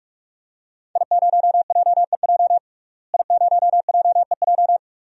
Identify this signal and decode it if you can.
Morse code, heard as a keyed tone, about 41 wpm, 700 Hz. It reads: I0JEJ I0JEJ
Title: DXCC-100 Call Signs - 5 Characters - Repeated 2x - 45wpm